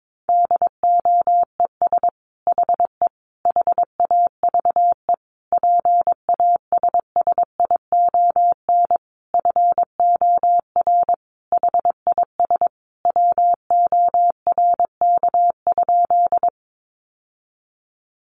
Morse code, heard as a keyed tone, about 22 wpm, 700 Hz. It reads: DOES HE HAVE PASSION FOR HIS WORK?
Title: Sentences from Top 1300 Words 22wpm